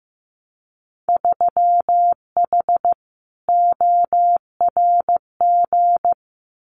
Morse code, heard as a keyed tone, about 15 wpm, 700 Hz. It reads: 3H ORG
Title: Winter Field Day Exchanges Encore 15wpm